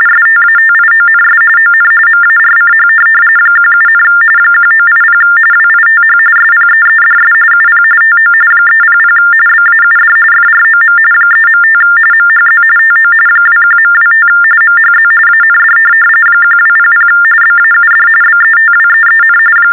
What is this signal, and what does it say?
TTY/TDD text telephone tones, about 45.45 baud. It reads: STATIONS ON MARCH 17, 1980 WITH SPEEDS UP TO 300 BAUD FROM 3.5 MHZ TO 21.25 MHZ AND 1200 BAUD BETWEEN
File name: teletype-RTTY-45